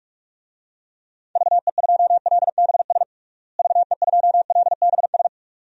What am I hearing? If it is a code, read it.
Morse code, heard as a keyed tone, about 46 wpm, 700 Hz. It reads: VE2LBS VE2LBS
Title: DXCC-100 Call Signs - 6 Characters - Repeated 2x - 45wpm